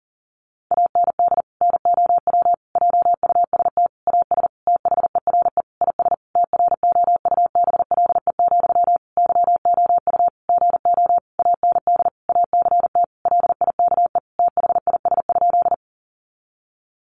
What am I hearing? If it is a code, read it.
Morse code, heard as a keyed tone, about 40 wpm, 700 Hz. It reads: AND NOW JUST AS THERE IS TROUBLE, YOU GO AND ACT LIKE THIS?
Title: Sentences from Top 1200 Words 40wpm